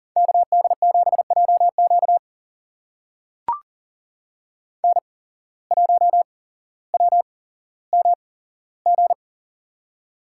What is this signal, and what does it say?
Morse code, heard as a keyed tone, about 42 wpm, spaced as for 17 wpm, 700 Hz. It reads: KD7JQ N1WMG
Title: USA Call Signs - 5 Characters - 8x Character Spacing - 40wpm